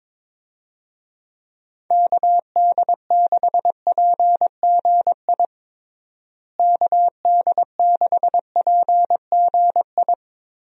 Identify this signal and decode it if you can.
Morse code, heard as a keyed tone, about 22 wpm, 700 Hz. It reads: KD6PGI KD6PGI
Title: USA Call Signs - 6 Characters - Repeated 3x - 22wpm